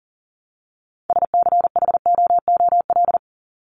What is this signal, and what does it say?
Morse code, heard as a keyed tone, about 40 wpm, 700 Hz. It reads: SCHOOL
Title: Top 300 Words 40wpm